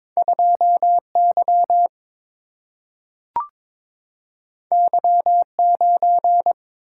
Morse code, heard as a keyed tone, about 22 wpm, 700 Hz. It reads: 2 Y Y 9
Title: Single-Letter Single-Number 22wpm